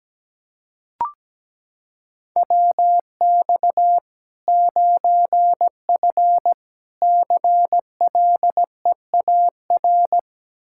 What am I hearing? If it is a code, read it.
Morse code, heard as a keyed tone, about 17 wpm, 700 Hz. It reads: WX 9F CLEAR
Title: K1USN SST QSO Practice 17wpm